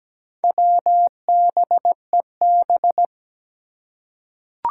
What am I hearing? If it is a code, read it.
Morse code, heard as a keyed tone, about 17 wpm, 700 Hz. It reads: WBEB
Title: Four Challenging Letter-Number Combinations 17wpm